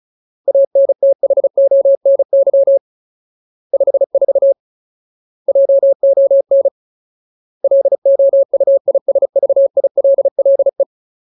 Morse code, heard as a keyed tone, about 34 wpm, 550 Hz. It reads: ANTHONY 54 JON LOUISVILLE
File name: Ultimate Word List / 35WPM Ultimate